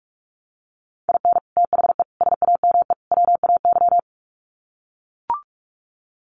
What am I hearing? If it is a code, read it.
Morse code, heard as a keyed tone, about 44 wpm, 700 Hz. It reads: IN THE SAME WAY
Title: Sets of 4 Words 45wpm